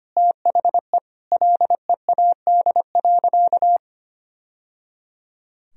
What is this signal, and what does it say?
Morse code, heard as a keyed tone, about 25 wpm, 700 Hz. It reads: THE LEAD.
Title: Sentences from Top 1000 Words - Review Part 2 - 25wpm